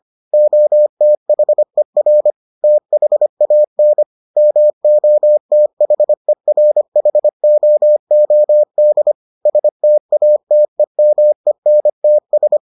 Morse code, heard as a keyed tone, about 25 wpm, 600 Hz. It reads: OTHER THAN MOTHERHOOD STATEMENTS